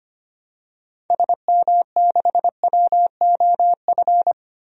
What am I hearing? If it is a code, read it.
Morse code, heard as a keyed tone, about 25 wpm, 700 Hz. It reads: SM6WOF
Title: DXCC-100 Call Signs - 6 Characters - 25wpm